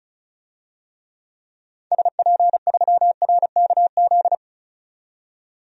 Morse code, heard as a keyed tone, about 36 wpm, 700 Hz. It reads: SP3RKZ
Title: DXCC-100 Call Signs - 6 Characters - 35wpm